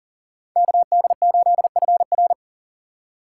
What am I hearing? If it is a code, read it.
Morse code, heard as a keyed tone, about 50 wpm, 700 Hz. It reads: KD8FR